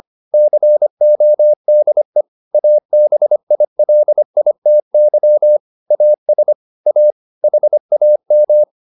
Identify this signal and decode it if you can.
Morse code, heard as a keyed tone, about 25 wpm, 600 Hz. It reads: CODE ABILITY AS A HAM